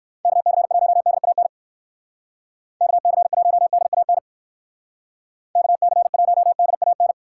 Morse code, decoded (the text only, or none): KC1DAN KC1DAN KC1DAN